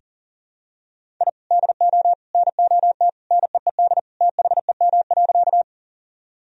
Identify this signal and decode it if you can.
Morse code, heard as a keyed tone, about 37 wpm, 700 Hz. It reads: I DO NOT NEED THEM.
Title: Sentences from Top 1000 Words - Review Part 1 - 40wpm